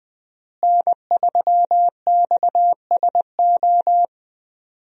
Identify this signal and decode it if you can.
Morse code, heard as a keyed tone, about 20 wpm, 700 Hz. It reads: N3XSO
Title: USA Call Signs - 5 Characters - Repeated 3x - 20wpm